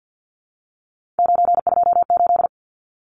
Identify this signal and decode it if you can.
Morse code, heard as a keyed tone, about 50 wpm, 700 Hz. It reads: 928
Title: Numbers - 3-Digit 50wpm